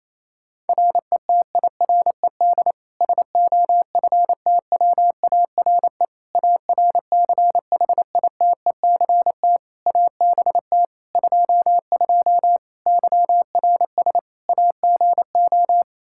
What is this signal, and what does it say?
Morse code, heard as a keyed tone, about 28 wpm, 700 Hz. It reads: RETIRED SOFTWARE ARCHITECT ABT 22 YRS AGO
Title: Practice QSO - Exchange 3 - More Info - 28wpm